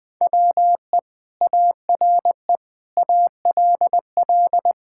Morse code, heard as a keyed tone, about 20 wpm, 700 Hz. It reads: WE ARE ALL
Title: Sentences with 2-Word Phrases - Spoken Beginning - 20wpm